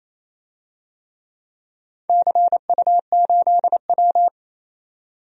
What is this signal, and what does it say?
Morse code, heard as a keyed tone, about 27 wpm, 700 Hz. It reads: CU8W
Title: DXCC-100 Call Signs - 4 Characters - 28wpm